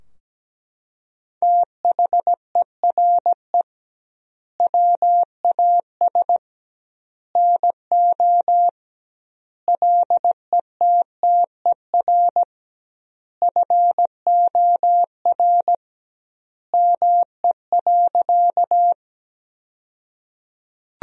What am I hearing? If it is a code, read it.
Morse code, heard as a keyed tone, about 17 wpm, 700 Hz. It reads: THERE WAS NO LETTER FOR ME.
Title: 2x Word-Spacing - Sentences from Top 300 Words - 17wpm